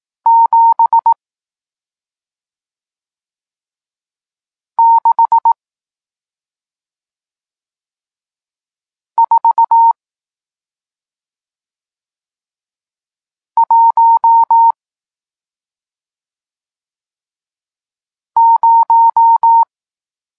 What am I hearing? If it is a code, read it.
Morse code, heard as a keyed tone, about 18 wpm, 920 Hz. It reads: 7 6 4 1 0
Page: Tekst 40 – Nauka znaków przestankowych